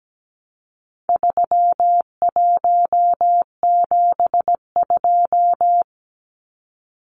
Morse code, heard as a keyed tone, about 17 wpm, 700 Hz. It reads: 3172
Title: Numbers - 4-Digit 17wpm